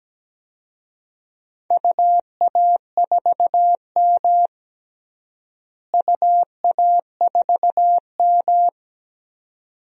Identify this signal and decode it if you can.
Morse code, heard as a keyed tone, about 17 wpm, 700 Hz. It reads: UA4M UA4M
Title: DXCC-100 Call Signs - 4 Characters - Repeated 2x - 17wpm